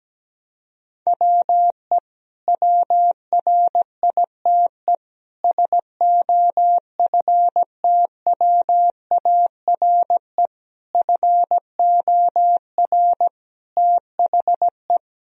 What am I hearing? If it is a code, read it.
Morse code, heard as a keyed tone, about 17 wpm, 700 Hz. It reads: WE WRITE SOFTWARE FOR THE
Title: Sentences from Top 1000 Words - Review Part 3 - 17wpm